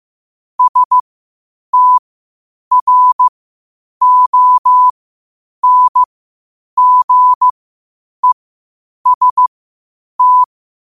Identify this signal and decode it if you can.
Morse code, heard as a keyed tone, about 15 wpm, spaced as for 8 wpm, 1000 Hz. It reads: STRONGEST
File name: qotdfiles / 8wpm